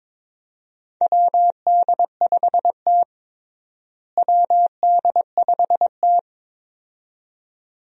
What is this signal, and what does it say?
Morse code, heard as a keyed tone, about 22 wpm, 700 Hz. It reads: WD5T WD5T
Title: USA Call Signs - 4 Characters - Repeated 2x - 22wpm